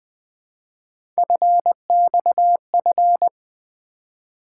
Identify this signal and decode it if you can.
Morse code, heard as a keyed tone, about 20 wpm, 700 Hz. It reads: FXF
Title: Three Letter-Number Combinations - 20wpm